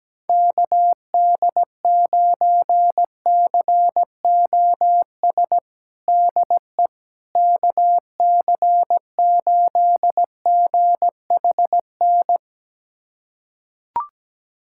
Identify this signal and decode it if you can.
Morse code, heard as a keyed tone, about 17 wpm, 700 Hz. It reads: KD9COS DE KC8GHN
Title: USA Callsign DE Callsign - 6 Characters - 17wpm